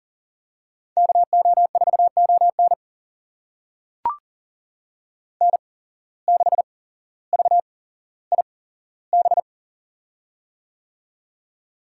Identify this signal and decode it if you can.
Morse code, heard as a keyed tone, about 40 wpm, spaced as for 17 wpm, 700 Hz. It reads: KO4ON N6VIB
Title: USA Call Signs - 5 Characters - 8x Character Spacing - 40wpm